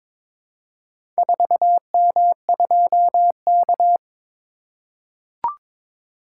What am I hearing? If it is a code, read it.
Morse code, heard as a keyed tone, about 22 wpm, 700 Hz. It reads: 4M2K
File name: practice / 22wpm / Four Letter-Number Combinations - 22wpm